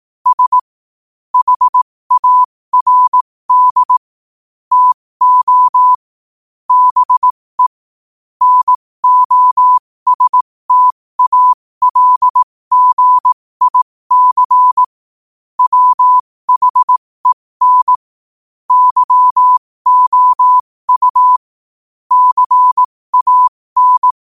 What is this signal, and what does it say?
Morse code, heard as a keyed tone, about 18 wpm, spaced as for 15 wpm, 1000 Hz. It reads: S HARD TO BE NOSTALGIC WHEN YOU CAN
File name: qotdfiles / 15wpm